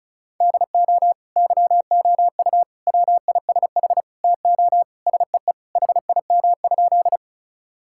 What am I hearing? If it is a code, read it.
Morse code, heard as a keyed tone, about 36 wpm, 700 Hz. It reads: DO YOU WISH TO SEE HIM?
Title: Sentences from Top 1600 Words 35wpm